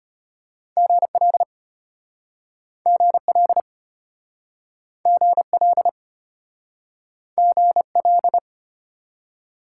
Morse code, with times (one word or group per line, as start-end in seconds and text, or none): 0.8-1.4 GL
2.9-3.6 GL
5.1-5.9 GL
7.4-8.4 GL